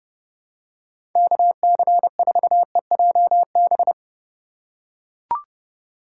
Morse code, heard as a keyed tone, about 32 wpm, 700 Hz. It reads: KC4EJB
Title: USA Call Signs - 6 Characters - 2x Character Spacing - 30wpm